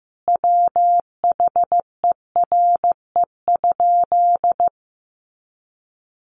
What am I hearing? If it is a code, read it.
Morse code, heard as a keyed tone, about 15 wpm, 700 Hz. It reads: WHERE?
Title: Sentences from Top 1200 Words 15wpm